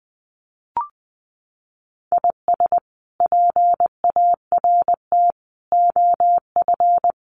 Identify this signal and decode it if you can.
Morse code, heard as a keyed tone, about 19 wpm, 700 Hz. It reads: IS PART OF
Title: Sets of 4 Words 20wpm